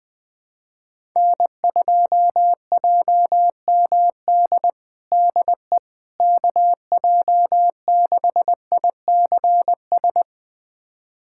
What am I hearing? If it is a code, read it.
Morse code, heard as a keyed tone, about 20 wpm, 700 Hz. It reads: N2JMD DE KJ6ICS
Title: USA Callsign DE Callsign 20wpm